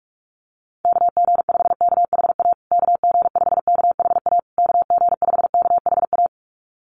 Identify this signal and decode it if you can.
Morse code, heard as a keyed tone, about 44 wpm, 700 Hz. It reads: KG5KHA KG5KHA KG5KHA
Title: USA Call Signs Repeated 3x 45wpm